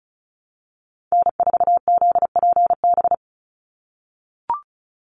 Morse code, heard as a keyed tone, about 36 wpm, 700 Hz. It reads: N4ZPB
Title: USA Call Signs 35wpm